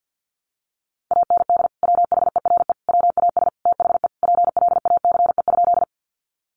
Morse code, heard as a keyed tone, about 48 wpm, 700 Hz. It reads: AND WHERE WAS THE PLACE?
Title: Sentences from Top 200 Words 50wpm